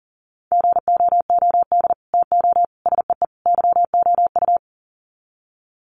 Morse code, heard as a keyed tone, about 38 wpm, 700 Hz. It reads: GOOD TO SEE YOU
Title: Sets of 4 Words 40wpm